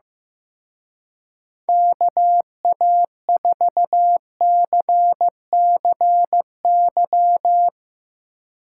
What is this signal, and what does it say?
Morse code, heard as a keyed tone, about 16 wpm, 700 Hz. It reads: KA4CCY
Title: Speed Racing - USA 2x3 Call Signs 15wpm